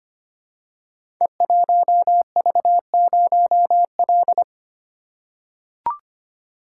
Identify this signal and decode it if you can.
Morse code, heard as a keyed tone, about 25 wpm, 700 Hz. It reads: E1V0L
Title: Code Groups - 5 Letter:Numbers - 25wpm